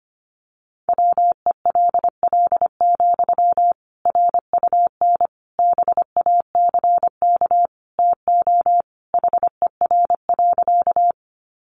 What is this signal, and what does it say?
Morse code, heard as a keyed tone, about 25 wpm, 700 Hz. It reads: WELL, RUN BACK TO HER.
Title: Sentences from Top 300 Words 25wpm